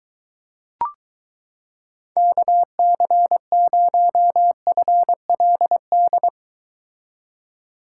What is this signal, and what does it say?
Morse code, heard as a keyed tone, about 22 wpm, 700 Hz. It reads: KC0FLD